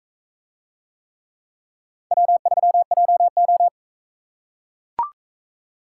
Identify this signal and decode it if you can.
Morse code, heard as a keyed tone, about 44 wpm, 700 Hz. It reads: W3JO